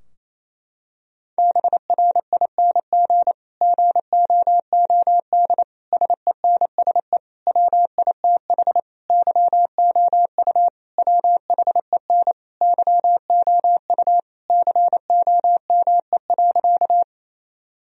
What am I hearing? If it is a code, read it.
Morse code, heard as a keyed tone, about 28 wpm, 700 Hz. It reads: BRING GOOD SENSE WITH YOU WHEN YOU COME.
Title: Sentences from Top 1000 Words - Review Part 4 - 28wpm